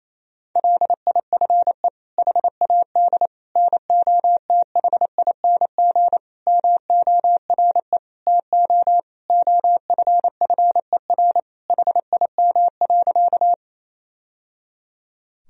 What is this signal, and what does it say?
Morse code, heard as a keyed tone, about 28 wpm, 700 Hz. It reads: LIFE HAD NOTHING MORE TO OFFER HIM.
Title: Sentences from Top 1000 Words - Review Part 4 - 28wpm